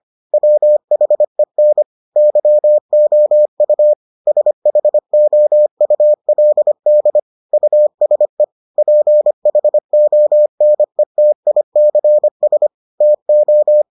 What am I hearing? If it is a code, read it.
Morse code, heard as a keyed tone, about 25 wpm, 600 Hz. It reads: WHEN YOU SHOULD USE PHONETICS TO